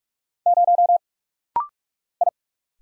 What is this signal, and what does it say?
Morse code, heard as a keyed tone, about 46 wpm, 700 Hz. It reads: 0 I
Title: Single Letter-Number - Rapid-Fire 45wpm